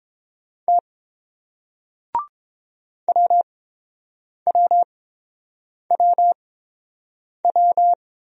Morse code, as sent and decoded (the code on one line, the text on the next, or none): - / .-- .-- .-- .--
T WWWW